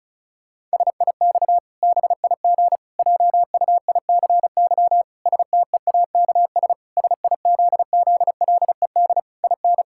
Morse code, heard as a keyed tone, about 36 wpm, 700 Hz. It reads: SIX BIG JUICY STEAKS SIZZLED IN
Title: Pangram Challenge v2 - 35wpm